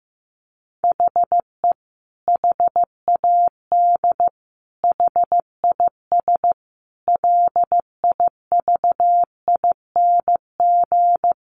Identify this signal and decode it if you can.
Morse code, heard as a keyed tone, about 15 wpm, 700 Hz. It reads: HE HAD HIS LIVING
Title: Sentences from Top 1500 Words 15wpm